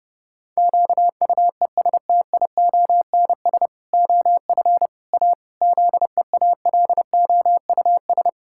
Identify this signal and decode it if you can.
Morse code, heard as a keyed tone, about 30 wpm, 700 Hz. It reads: QUESTIONS OF A ZEALOUS
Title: Pangram Challenge v2 - 30wpm